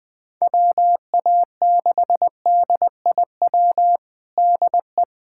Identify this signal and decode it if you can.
Morse code, heard as a keyed tone, about 20 wpm, 700 Hz. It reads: WA6DIW DE
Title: USA Callsign DE Callsign 20wpm